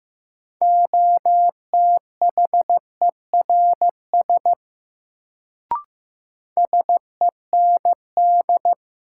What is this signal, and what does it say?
Morse code, heard as a keyed tone, about 15 wpm, 700 Hz. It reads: OTHERS SEND
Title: Sentences from Top 1500 Words - New Word Drill Alt - 15wpm